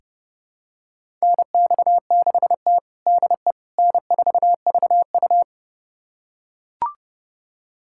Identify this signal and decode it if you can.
Morse code, heard as a keyed tone, about 32 wpm, 700 Hz. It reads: NX6T DE N4VU
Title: USA Callsign DE Callsign - 4 Characters - 30wpm